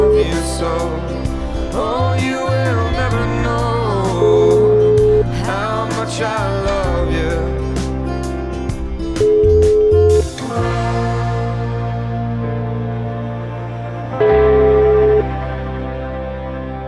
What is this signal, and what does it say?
Ringback tone (Europe, 425 Hz), with noise.